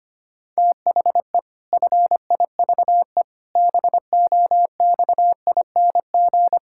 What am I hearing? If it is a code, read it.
Morse code, heard as a keyed tone, about 25 wpm, 700 Hz. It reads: THE FIVE BOXING
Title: Pangram Challenge v2 - 25wpm